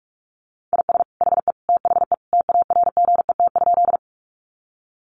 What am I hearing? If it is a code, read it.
Morse code, heard as a keyed tone, about 46 wpm, 700 Hz. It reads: IS HE THE TARGET?
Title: Sentences from Top 1200 Words 45wpm